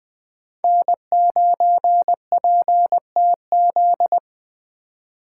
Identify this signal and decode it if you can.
Morse code, heard as a keyed tone, about 20 wpm, 700 Hz. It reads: N9PTZ
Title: Speed Racing - USA 1x3 Call Signs 17wpm